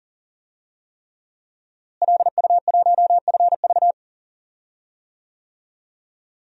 Morse code, heard as a keyed tone, about 39 wpm, 700 Hz. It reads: LU1FV
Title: DXCC-100 Call Signs - 5 Characters - Repeated 3x - 40wpm